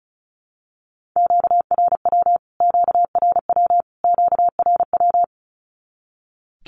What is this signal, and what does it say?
Morse code, heard as a keyed tone, about 34 wpm, 700 Hz. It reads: QRW QRW QRW
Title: Q-Codes Repeated 3x 35wpm